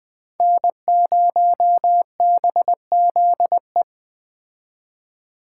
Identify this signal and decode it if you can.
Morse code, heard as a keyed tone, about 20 wpm, 700 Hz. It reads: N0BZE